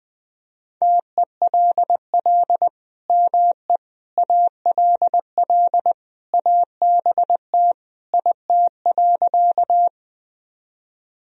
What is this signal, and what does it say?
Morse code, heard as a keyed tone, about 21 wpm, 700 Hz. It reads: TELL ME ALL ABT IT.
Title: Sentences from Top 200 Words and Abbreviations 20wpm